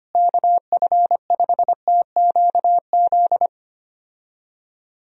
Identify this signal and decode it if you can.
Morse code, heard as a keyed tone, about 25 wpm, 700 Hz. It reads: KF5TQZ
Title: USA Call Signs - 6 Characters - Repeated 2x - 25wpm